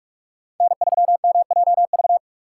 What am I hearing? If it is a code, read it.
Morse code, heard as a keyed tone, about 44 wpm, 700 Hz. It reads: N3MJV
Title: USA Call Signs - 6x Character Spacing - 45wpm